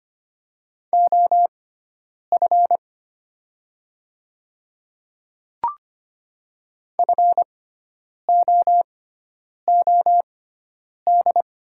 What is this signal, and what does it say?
Morse code, heard as a keyed tone, about 25 wpm, spaced as for 9 wpm, 700 Hz. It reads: OF FOOD
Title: 6x Character Spacing - Top 300 Words - 25wpm